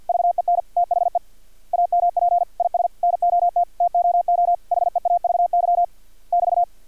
Morse code, heard as a keyed tone, about 50 wpm, 700 Hz. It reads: VEN THE AMP IS NOT TOO HEAVY <BT>
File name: archived / CRSnet